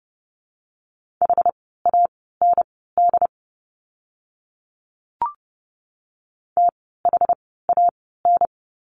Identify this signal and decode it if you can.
Morse code, heard as a keyed tone, about 30 wpm, spaced as for 17 wpm, 700 Hz. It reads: HAND THAN
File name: course / 30:10wpm / 115 Lesson 19f - Letter w - Words 30wpm